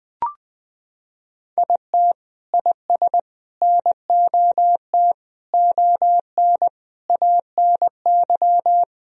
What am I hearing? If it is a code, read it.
Morse code, heard as a keyed tone, about 19 wpm, 700 Hz. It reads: IT IS NOT ON ANY
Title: Sentences from Top 1000 Words - Review Part 2 - 20wpm